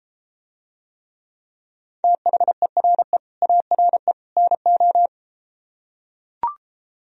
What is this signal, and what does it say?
Morse code, heard as a keyed tone, about 34 wpm, 700 Hz. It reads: THERE ARE NO